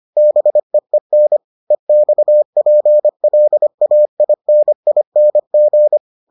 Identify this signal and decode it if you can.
Morse code, heard as a keyed tone, about 25 wpm, 600 Hz. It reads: BEEN EXPLAINING